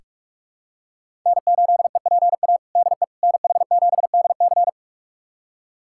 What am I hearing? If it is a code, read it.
Morse code, heard as a keyed tone, about 46 wpm, 700 Hz. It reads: N8EPA DE NH7DC
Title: USA Callsign DE Callsign - 5 Characters - 45wpm